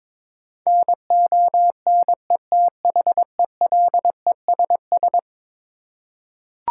Morse code, heard as a keyed tone, about 22 wpm, 700 Hz. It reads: NONETHELESS
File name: practice / 22wpm / Eleven Letter Words - 22wpm